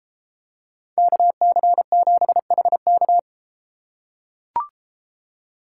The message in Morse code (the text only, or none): KC7HK